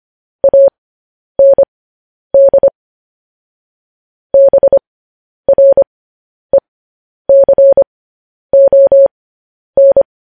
Morse code, heard as a keyed tone, about 25 wpm, spaced as for 10 wpm, 550 Hz. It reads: AND BRECON
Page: news-headlines-in-morse-code-at-10-wpm